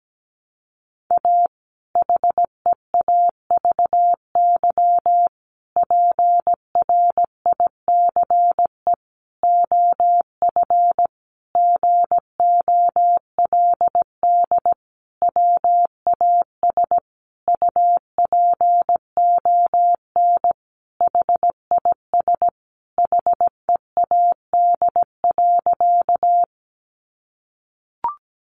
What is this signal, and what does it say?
Morse code, heard as a keyed tone, about 17 wpm, 700 Hz. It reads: A HEAVY PRICE OF GOLD WAS UPON HIS HEAD.
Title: Sentences from Top 1100 Words 17wpm